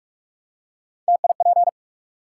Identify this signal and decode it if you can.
Morse code, heard as a keyed tone, about 44 wpm, 700 Hz. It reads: TIP
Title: Three Letter Words - 45wpm